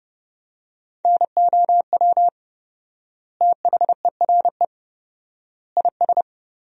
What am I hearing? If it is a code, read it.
Morse code, heard as a keyed tone, about 30 wpm, 700 Hz. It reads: NOW THERE IS
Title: 4x Word-Spacing - Sentences from Top 100 Words - 30wpm